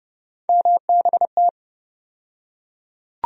Morse code, heard as a keyed tone, about 30 wpm, 700 Hz. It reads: MBT
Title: USA Call Sign Postfixes - 3 Letters 30wpm